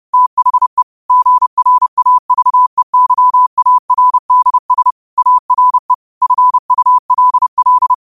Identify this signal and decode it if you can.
Morse code, heard as a keyed tone, about 30 wpm, 1000 Hz. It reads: THE GRAVEYARDS ARE FULL
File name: qotdfiles / 30wpm